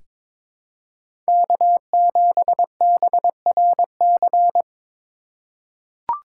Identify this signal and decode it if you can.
Morse code, heard as a keyed tone, about 22 wpm, 700 Hz. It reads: K7BRC
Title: USA Call Signs - 5 Characters - 3x Character Spacing - 22wpm